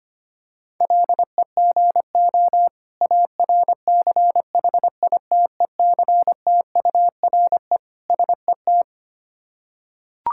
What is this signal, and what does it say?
Morse code, heard as a keyed tone, about 25 wpm, 690 Hz. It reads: LEGO ARCHITECTURE SET
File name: practice / 25wpm / Difficult Words with Hints - 25wpm